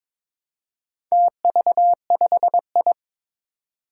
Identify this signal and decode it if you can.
Morse code, heard as a keyed tone, about 22 wpm, 700 Hz. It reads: TV5I
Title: Four Challenging Letter-Number Combinations 22wpm